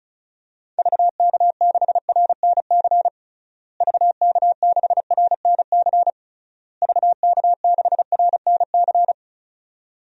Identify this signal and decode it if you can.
Morse code, heard as a keyed tone, about 34 wpm, 700 Hz. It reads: VK6RNC VK6RNC VK6RNC
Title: DXCC-100 Call Signs - 6 Characters - Repeated 3x - 35wpm